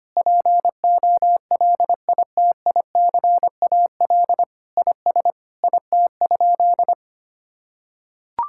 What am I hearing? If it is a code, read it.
Morse code, heard as a keyed tone, about 25 wpm, 700 Hz. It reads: POLITICAL IS IT?
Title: Sentences from Top 1000 Words - Review Part 5 - 25wpm